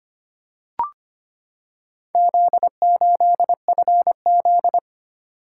Morse code, heard as a keyed tone, about 25 wpm, 700 Hz. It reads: Z8FZ
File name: practice / 25wpm / Four Letter-Number Combinations - 25wpm